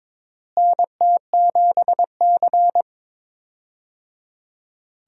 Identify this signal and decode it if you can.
Morse code, heard as a keyed tone, about 22 wpm, 700 Hz. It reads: NT7C
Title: USA Call Signs - 4 Characters - Repeated 3x - 22wpm